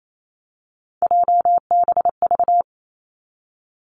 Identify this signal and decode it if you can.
Morse code, heard as a keyed tone, about 28 wpm, 700 Hz. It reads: JBV
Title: Three Challenging Letter-Number Combinations 28wpm